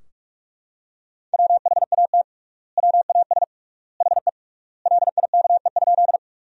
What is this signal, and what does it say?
Morse code, heard as a keyed tone, about 45 wpm, 700 Hz. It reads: WHAT WAS HE LIKE?
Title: 3x Word-Spacing - Sentences from Top 100 Words - 45wpm